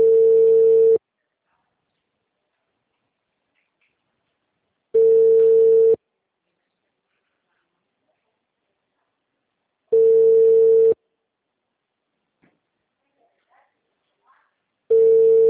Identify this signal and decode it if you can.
Ringback tone (China, 450 Hz), with speech.